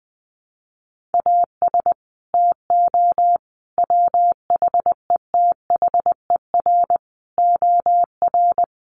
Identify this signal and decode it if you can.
Morse code, heard as a keyed tone, about 20 wpm, 700 Hz. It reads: AS TO WHETHER OR
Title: Sets of 5 Words Encore 20wpm